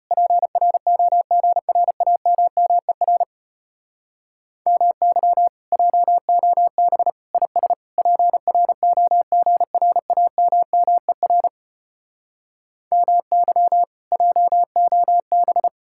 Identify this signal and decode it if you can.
Morse code, heard as a keyed tone, about 33 wpm, 700 Hz. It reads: PROGRAMMER MY JOB IS PROGRAMMER MY JOB